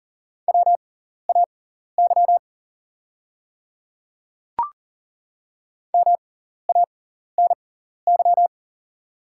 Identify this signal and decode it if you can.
Morse code, heard as a keyed tone, about 39 wpm, spaced as for 14 wpm, 700 Hz. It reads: WAY MANY
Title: 6x Character Spacing - Top 100 Words - 40wpm